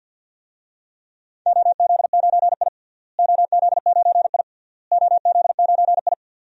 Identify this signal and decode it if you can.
Morse code, heard as a keyed tone, about 46 wpm, 700 Hz. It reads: OZ9I OZ9I OZ9I
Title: DXCC-100 Call Signs - 4 Characters - Repeated 3x - 50wpm